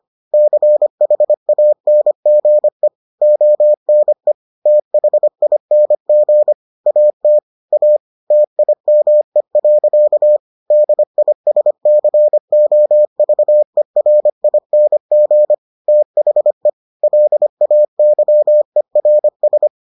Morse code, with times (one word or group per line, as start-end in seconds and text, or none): 0.3-2.9 CHANGE
3.2-4.3 ONE
4.7-6.5 THING
6.9-7.4 AT
7.7-8.0 A
8.3-10.4 TIME.
10.7-15.5 DISCOVERING
15.9-16.7 THE
17.0-19.7 LAYERS